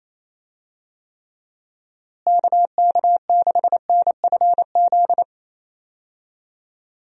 Morse code, heard as a keyed tone, about 28 wpm, 700 Hz. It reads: KK6NFZ
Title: USA Call Signs - 6 Characters - Repeated 2x - 28wpm